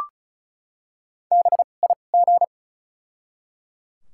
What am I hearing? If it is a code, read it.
Morse code, heard as a keyed tone, about 34 wpm, spaced as for 25 wpm, 700 Hz. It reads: BIG
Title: 2x Character Spacing - Top 200 Words - 35wpm